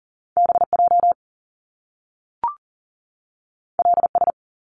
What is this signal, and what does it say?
Morse code, heard as a keyed tone, about 39 wpm, 700 Hz. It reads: BJ LS
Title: Top Two Letter Combinations in Words 40wpm